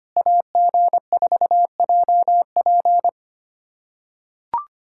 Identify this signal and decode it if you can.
Morse code, heard as a keyed tone, about 25 wpm, 690 Hz. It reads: AG4JP
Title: SOTA QSO Practice 25wpm